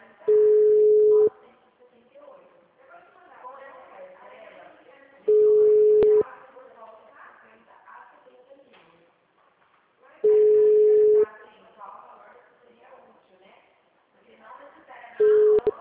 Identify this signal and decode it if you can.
Ringback tone (Europe, 425 Hz), with speech.